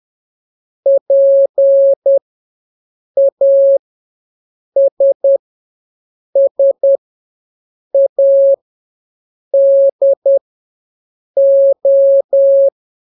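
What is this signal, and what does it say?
Morse code, heard as a keyed tone, about 10 wpm, spaced as for 6 wpm, 550 Hz. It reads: PASSADO
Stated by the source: Telegraphy radio challenge for scouts in the Jamboree-On-The-Air.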